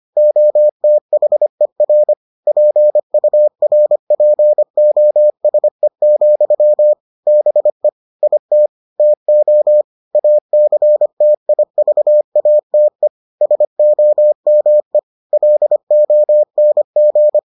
Morse code, heard as a keyed tone, about 25 wpm, 600 Hz. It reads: OTHER PURPOSE, BE IT TO ACTIVATE SOME LONG